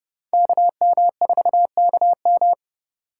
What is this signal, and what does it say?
Morse code, heard as a keyed tone, about 37 wpm, 700 Hz. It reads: KM4KM
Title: Speed Racing - USA Call Signs 30wpm